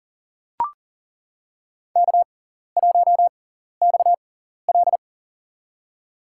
Morse code, heard as a keyed tone, about 37 wpm, spaced as for 14 wpm, 700 Hz. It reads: K1XL
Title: USA Call Signs - 4 Characters - 6x Character Spacing - 40wpm